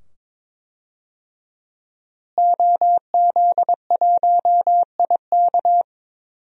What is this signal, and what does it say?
Morse code, heard as a keyed tone, about 22 wpm, 700 Hz. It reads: OZ1IK
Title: DXCC-100 Call Signs - 5 Characters - Repeated 2x - 22wpm